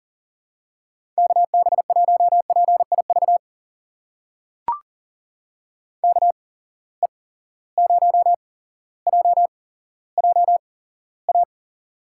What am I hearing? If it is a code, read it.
Morse code, heard as a keyed tone, about 40 wpm, spaced as for 17 wpm, 700 Hz. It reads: KB1PIV KE0JJA
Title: USA Call Signs - 6 Characters - 8x Character Spacing - 40wpm